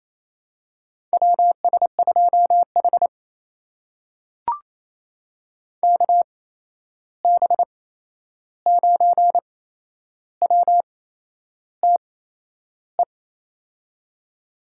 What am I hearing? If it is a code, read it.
Morse code, heard as a keyed tone, about 27 wpm, spaced as for 10 wpm, 700 Hz. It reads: WS2H KB9WTE
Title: USA Call Signs - 8x Character Spacing - 28wpm